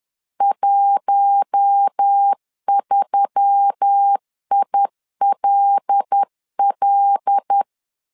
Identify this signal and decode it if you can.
Morse code, heard as a keyed tone, about 10 wpm, 800 Hz. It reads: 13ILL
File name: LEZIONI / 5 LBVC / 5 L 10wpm 2259